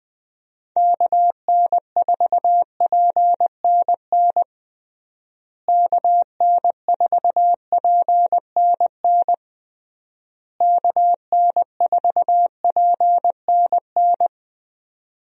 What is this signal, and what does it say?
Morse code, heard as a keyed tone, about 20 wpm, 700 Hz. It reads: KN4PNN KN4PNN KN4PNN
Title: USA Call Signs - 6 Characters - Repeated 3x - 20wpm